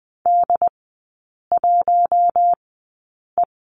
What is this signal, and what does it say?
Morse code, heard as a keyed tone, about 19 wpm, spaced as for 8 wpm, 700 Hz. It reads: D1E
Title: 100 Lesson 17e - Letter u - Three Characters 20wpm